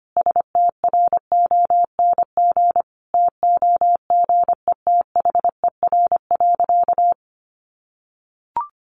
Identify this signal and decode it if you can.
Morse code, heard as a keyed tone, about 25 wpm, 700 Hz. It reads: STRONG TOGETHER.
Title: Sentences from Top 500 Words 25wpm